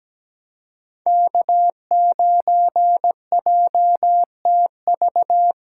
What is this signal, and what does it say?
Morse code, heard as a keyed tone, about 17 wpm, 700 Hz. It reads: K9JTV
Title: USA Callsign DE Callsign - 5 Characters - 17wpm